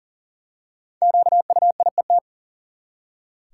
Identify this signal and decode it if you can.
Morse code, heard as a keyed tone, about 40 wpm, 700 Hz. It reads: QUIET
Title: Five Letter Words - 40wpm